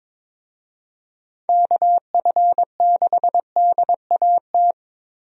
Morse code, heard as a keyed tone, about 22 wpm, 700 Hz. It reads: KF6DAT
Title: USA Call Signs - 6 Characters - 22wpm